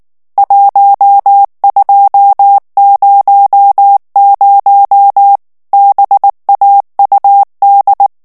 Morse code, heard as a keyed tone, about 19 wpm, 800 Hz. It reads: 1200 BAUD
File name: ANGLAIS texte clair 3 vitesse 18 mots minute